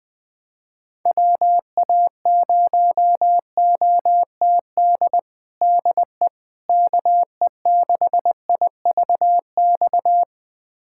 Morse code, heard as a keyed tone, about 21 wpm, 700 Hz. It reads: WA0OTD DE KE6IVX
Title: USA Callsign DE Callsign 20wpm